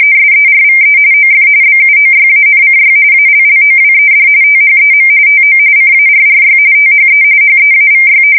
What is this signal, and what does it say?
RTTY, about 45.45 baud. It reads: SERVE OF BUOYANCY ALLOWED THEM TO CARRY THE HEAVY
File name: fsk170 / battleshipnj